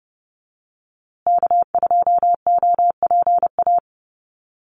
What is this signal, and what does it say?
Morse code, heard as a keyed tone, about 30 wpm, 700 Hz. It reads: K2OPA
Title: USA Call Signs Repeated 3x 30wpm